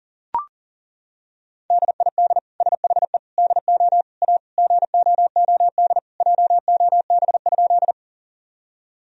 Morse code, heard as a keyed tone, about 40 wpm, 700 Hz. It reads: DID SHE DO A GOOD JOB?
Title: Sentences from Top 1000 Words - Review Part 3 - 40wpm